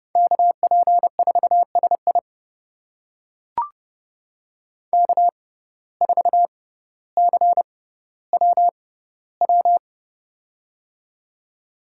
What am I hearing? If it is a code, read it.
Morse code, heard as a keyed tone, about 30 wpm, spaced as for 15 wpm, 700 Hz. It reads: KP4SI K4CWW
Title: USA Call Signs - 5 Characters - 6x Character Spacing - 30wpm